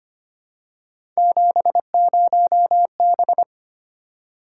Morse code, heard as a keyed tone, about 25 wpm, 700 Hz. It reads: 70B
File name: Three Letter-Number Combinations - 25wpm